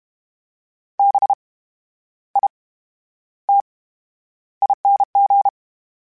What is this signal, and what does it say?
Morse code, heard as a keyed tone, about 32 wpm, spaced as for 11 wpm, 800 Hz. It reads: BITING
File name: Special INGs / Special INGs-mp3s / VerySlowToFaster Ing @30